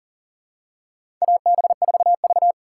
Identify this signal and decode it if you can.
Morse code, heard as a keyed tone, about 40 wpm, 700 Hz. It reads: AB4V
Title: USA Call Signs - 4 Characters - 6x Character Spacing - 40wpm